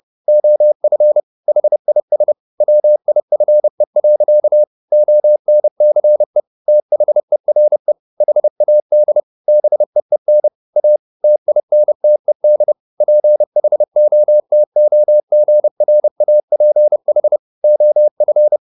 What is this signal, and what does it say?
Morse code, heard as a keyed tone, about 30 wpm, 600 Hz. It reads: OF HIS WIFE. ONCE THERE HAD BEEN A TINTED PHOTOGRAPH OF
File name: the old man and the sea / the old man and the sea 1 30wpm